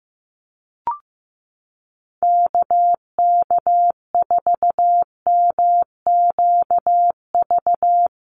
Morse code, heard as a keyed tone, about 15 wpm, 700 Hz. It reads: KK4MQV
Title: USA Call Signs 15wpm